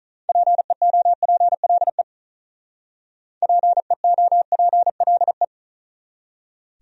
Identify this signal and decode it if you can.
Morse code, heard as a keyed tone, about 44 wpm, 700 Hz. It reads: PEOPLE PEOPLE